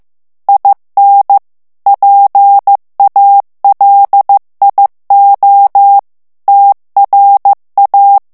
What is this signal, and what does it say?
Morse code, heard as a keyed tone, about 15 wpm, 800 Hz. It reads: IN PALIO TRA
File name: morse mp3 / ITALIEN texte clair 2 vitesse 14 mots minute